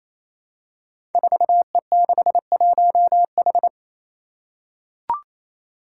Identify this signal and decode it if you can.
Morse code, heard as a keyed tone, about 27 wpm, 700 Hz. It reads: 4E61H
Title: Five Challenging Letter-Number Combinations 28wpm